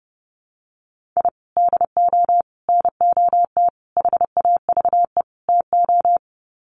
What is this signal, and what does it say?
Morse code, heard as a keyed tone, about 28 wpm, 700 Hz. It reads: I DO NOT HAVE TO
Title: Sets of 5 Words 30wpm